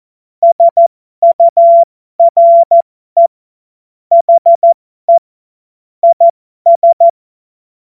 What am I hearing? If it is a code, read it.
Morse code, heard as a keyed tone, about 14 wpm, spaced as for 12 wpm, 680 Hz. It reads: SURE HE IS
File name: Morsekurs / Witze